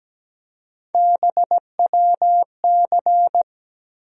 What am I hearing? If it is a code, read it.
Morse code, heard as a keyed tone, about 17 wpm, 700 Hz. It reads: BWC
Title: USA Call Sign Postfixes - 3 Letters 17wpm